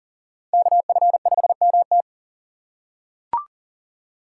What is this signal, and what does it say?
Morse code, heard as a keyed tone, about 40 wpm, 700 Hz. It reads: KF5MT
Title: USA Call Signs - 5 Characters - 3x Character Spacing - 40wpm